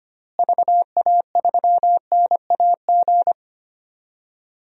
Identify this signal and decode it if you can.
Morse code, heard as a keyed tone, about 25 wpm, 700 Hz. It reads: VA3NAG
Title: DXCC-100 Call Signs - 6 Characters - Repeated 3x - 25wpm